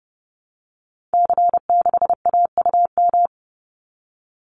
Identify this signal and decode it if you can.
Morse code, heard as a keyed tone, about 30 wpm, 700 Hz. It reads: C6AUM
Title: Call Signs Repeated 3x - 30wpm